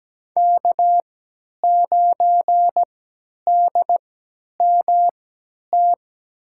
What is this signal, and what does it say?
Morse code, heard as a keyed tone, about 17 wpm, spaced as for 10 wpm, 700 Hz. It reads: K9DMT
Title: USA Call Signs - 5 Characters - 3x Character Spacing - 17wpm